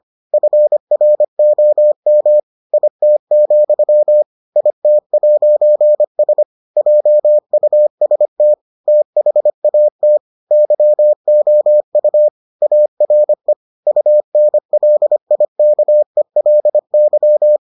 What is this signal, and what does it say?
Morse code, heard as a keyed tone, about 25 wpm, 600 Hz. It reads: FROM IT, IT'S JUST THAT YOU ARE UNLIKELY